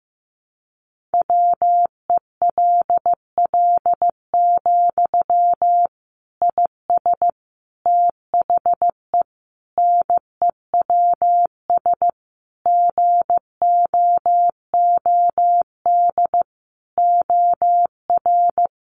Morse code, heard as a keyed tone, about 15 wpm, 700 Hz. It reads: WELL, IS THE NEWS GOOD OR
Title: Sentences from Top 800 Words 15wpm